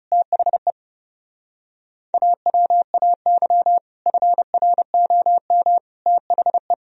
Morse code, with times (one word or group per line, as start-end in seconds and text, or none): 0.1-0.7 THE
2.1-3.8 AWAY
4.1-5.8 FROM
6.1-6.7 THE